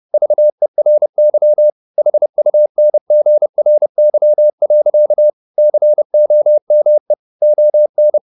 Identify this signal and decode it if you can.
Morse code, heard as a keyed tone, about 30 wpm, 600 Hz. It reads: VERY HUNGRY. COME ON
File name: the old man and the sea / the old man and the sea 1 30wpm